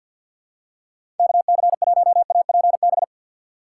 Morse code, heard as a keyed tone, about 46 wpm, 690 Hz. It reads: KC1APB
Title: USA Call Signs - 6 Characters - 3x Character Spacing - 50wpm